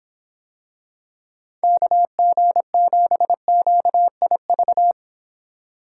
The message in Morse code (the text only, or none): KG7QIV